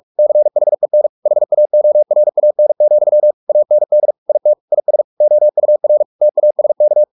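Morse code, heard as a keyed tone, about 44 wpm, 600 Hz. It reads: CHEN HAORAN, AND IT IS OUR TASK
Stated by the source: Morse practice at 45 wpm, the novel Never by Ken Follett in English, from CW med Gnister.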